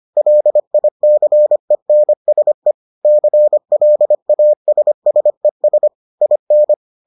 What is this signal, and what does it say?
Morse code, heard as a keyed tone, about 25 wpm, 600 Hz. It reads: LICENSE CLASSES IN